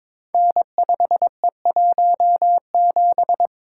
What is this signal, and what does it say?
Morse code, heard as a keyed tone, about 22 wpm, 700 Hz. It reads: N5E17
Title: Code Groups - 5 Letter:Numbers - 22wpm